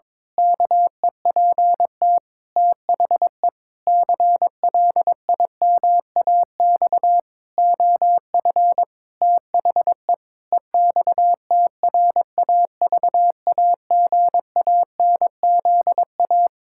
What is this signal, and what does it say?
Morse code, heard as a keyed tone, about 22 wpm, 700 Hz. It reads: KEPT THE CLIMAX OF THE EXTRAVAGANZA
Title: Pangram Challenge v2 - 22wpm